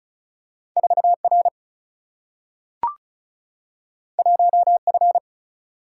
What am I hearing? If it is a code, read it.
Morse code, heard as a keyed tone, about 36 wpm, 700 Hz. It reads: 4R 1F
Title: Single-Letter Single-Number 35wpm